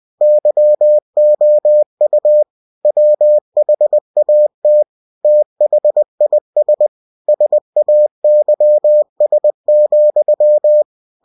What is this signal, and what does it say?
Morse code, heard as a keyed tone, about 20 wpm, 600 Hz. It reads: YOU WHAT THIS SAYS,
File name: cw-books / winnie-the-pooh / winnie-20wpm-page-0052